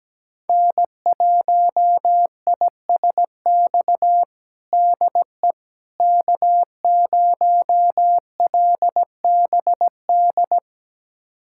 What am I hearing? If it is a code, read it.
Morse code, heard as a keyed tone, about 17 wpm, 700 Hz. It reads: N1ISX DE K0LBD
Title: USA Callsign DE Callsign - 5 Characters - 17wpm